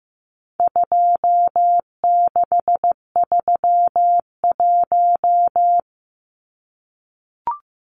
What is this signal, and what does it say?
Morse code, heard as a keyed tone, about 15 wpm, 700 Hz. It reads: 2631
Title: Numbers - 4-Digit 15wpm